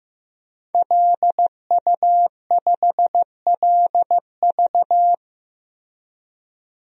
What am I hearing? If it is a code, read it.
Morse code, heard as a keyed tone, about 15 wpm, 700 Hz. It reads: LU5LV
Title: DXCC-100 Call Signs - 5 Characters - Repeated 3x - 15wpm